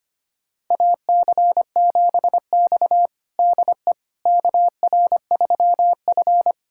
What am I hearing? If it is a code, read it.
Morse code, heard as a keyed tone, about 25 wpm, 700 Hz. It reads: AC7X DE KR3F
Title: USA Callsign DE Callsign - 4 Characters - 25wpm